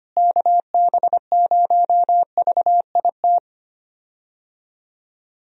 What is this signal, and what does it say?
Morse code, heard as a keyed tone, about 25 wpm, 700 Hz. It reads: KB0VIT
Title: USA Call Signs - 6 Characters - Repeated 3x - 25wpm